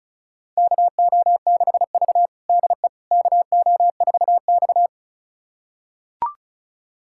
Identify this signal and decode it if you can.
Morse code, heard as a keyed tone, about 36 wpm, 700 Hz. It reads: KO6V DE KO4X
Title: USA Callsign DE Callsign - 4 Characters - 35wpm